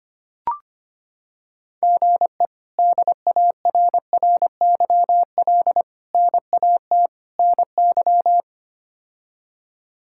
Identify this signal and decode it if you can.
Morse code, heard as a keyed tone, about 25 wpm, 700 Hz. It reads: GE DARRYL NAT NY
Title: K1USN SST QSO Practice - 25wpm